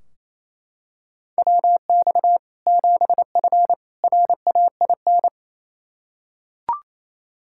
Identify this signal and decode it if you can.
Morse code, heard as a keyed tone, about 28 wpm, 700 Hz. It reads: WX 7F RAIN
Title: K1USN SST QSO Practice 28wpm